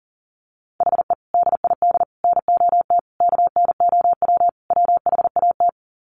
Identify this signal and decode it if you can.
Morse code, heard as a keyed tone, about 39 wpm, 700 Hz. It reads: HE DID NOT KNOW WHAT
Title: Sets of 5 Words Encore 40wpm